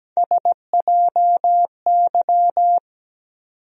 Morse code, heard as a keyed tone, about 17 wpm, 700 Hz. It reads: SJY
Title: USA Call Sign Postfixes - 3 Letters 17wpm